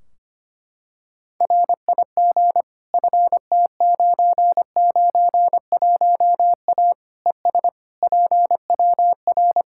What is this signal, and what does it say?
Morse code, heard as a keyed tone, about 25 wpm, 700 Hz. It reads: RIG FT991A ES PWR
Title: Practice QSO - Exchange 2 - Info - 25wpm